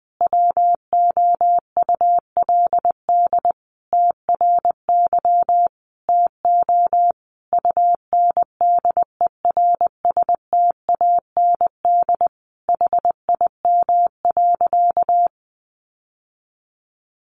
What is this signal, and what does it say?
Morse code, heard as a keyed tone, about 20 wpm, 700 Hz. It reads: WOULD TRY TO UNDERSTAND HIM.
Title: Sentences from Top 600 Words 20wpm